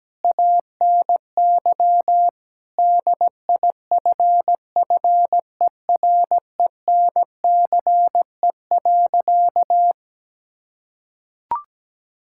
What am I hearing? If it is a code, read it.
Morse code, heard as a keyed tone, about 17 wpm, 700 Hz. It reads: ANY DIFFERENCE.
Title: Sentences from Top 1000 Words - Review Part 5 - 17wpm